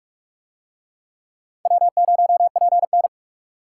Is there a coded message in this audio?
Yes: Morse code from 1.7 to 3.1 s, heard as a keyed tone, about 42 wpm, 700 Hz. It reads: W0PN